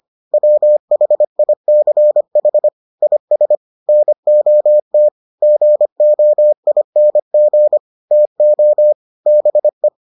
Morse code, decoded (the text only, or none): WHICH IS NOT GOING TO BE